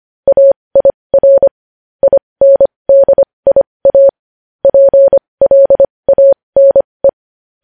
Morse code, heard as a keyed tone, about 25 wpm, spaced as for 20 wpm, 550 Hz. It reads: AIR INDIA PLANE
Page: News headlines in Morse code at 20 wpm